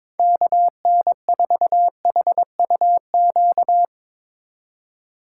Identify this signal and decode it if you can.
Morse code, heard as a keyed tone, about 22 wpm, 700 Hz. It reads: KN4HUQ
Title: USA Call Signs - 2x Character Spacing - 22wpm